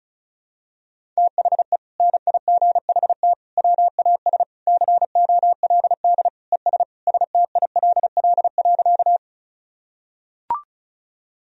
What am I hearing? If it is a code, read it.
Morse code, heard as a keyed tone, about 36 wpm, 700 Hz. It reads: THE NIGHT WAS COLD ES STILL.
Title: Sentences from Top 400 Words and Abbreviations 35wpm